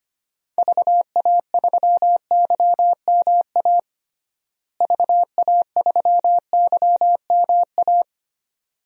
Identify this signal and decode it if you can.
Morse code, heard as a keyed tone, about 25 wpm, 700 Hz. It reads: VA3YMA VA3YMA
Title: DXCC-100 Call Signs - 6 Characters - Repeated 2x - 25wpm